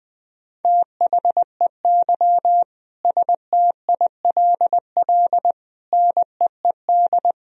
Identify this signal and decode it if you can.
Morse code, heard as a keyed tone, about 21 wpm, 700 Hz. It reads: THEY STILL NEED
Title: Sentences from Top 1000 Words - Review Part 4 - 20wpm